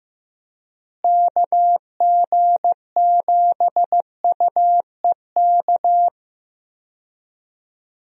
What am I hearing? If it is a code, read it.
Morse code, heard as a keyed tone, about 15 wpm, 700 Hz. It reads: KG7UEK
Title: USA Call Signs - 6 Characters - Repeated 3x - 15wpm